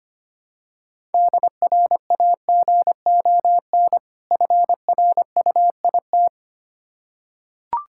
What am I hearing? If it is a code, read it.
Morse code, heard as a keyed tone, about 25 wpm, 690 Hz. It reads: DRAGON FRUIT
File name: practice / 25wpm / Difficult Words with Hints - 25wpm